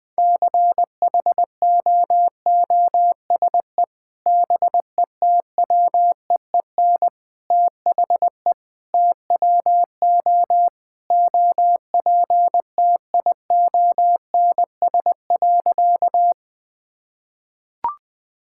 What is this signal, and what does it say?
Morse code, heard as a keyed tone, about 21 wpm, 700 Hz. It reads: CHOOSE BETWEEN THE TWO OPTIONS.
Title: Sentences with 3-Word Phrases - 20wpm